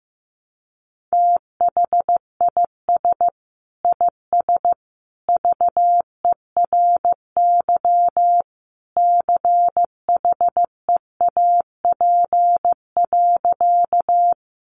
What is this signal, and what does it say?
Morse code, heard as a keyed tone, about 15 wpm, 700 Hz. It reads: THIS IS VERY CHEAP.
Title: Sentences from Top 1200 Words 15wpm